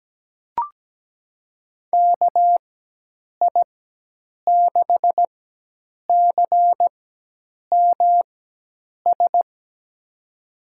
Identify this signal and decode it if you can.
Morse code, heard as a keyed tone, about 17 wpm, spaced as for 8 wpm, 700 Hz. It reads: KI6CMS
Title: USA Call Signs - 6 Characters - 4x Character Spacing - 17wpm